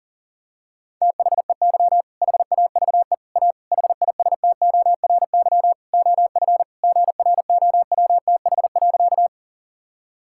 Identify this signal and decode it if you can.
Morse code, heard as a keyed tone, about 40 wpm, 700 Hz. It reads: THEY HAVE A HISTORY OF GROWTH.
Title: Sentences from Top 1000 Words - Review Part 5 - 40wpm